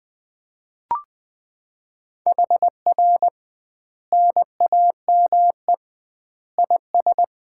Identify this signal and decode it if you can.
Morse code, heard as a keyed tone, about 21 wpm, 700 Hz. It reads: HR NAME IS
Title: Common QSO Phrases - 2x Word-Spacing - 20wpm